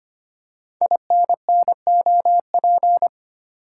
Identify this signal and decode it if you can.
Morse code, heard as a keyed tone, about 25 wpm, 700 Hz. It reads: INNOP
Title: Code Groups - 5 Letters - 25wpm